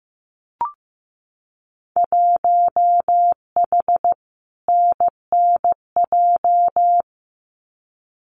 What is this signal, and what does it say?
Morse code, heard as a keyed tone, about 15 wpm, 700 Hz. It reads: 1H NNJ
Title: Winter Field Day Exchanges 15wpm